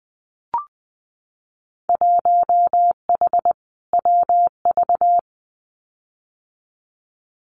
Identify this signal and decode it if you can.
Morse code, heard as a keyed tone, about 20 wpm, 700 Hz. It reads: 1H WV
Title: Winter Field Day Exchanges Encore 20wpm